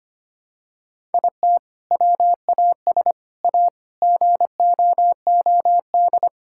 Morse code, heard as a keyed tone, about 25 wpm, 700 Hz. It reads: IT WAS A GOOD
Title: Sentences from Top 1000 Words - Review Part 5 - 25wpm